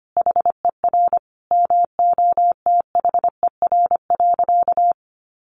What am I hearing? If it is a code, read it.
Morse code, heard as a keyed tone, about 25 wpm, 700 Hz. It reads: HER MOTHER.
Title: Sentences from Top 400 Words 25wpm